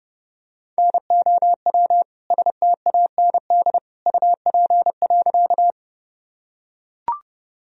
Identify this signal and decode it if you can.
Morse code, heard as a keyed tone, about 32 wpm, 700 Hz. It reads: NOW STAND UP.
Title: Sentences from Top 1000 Words - Review Part 1 - 30wpm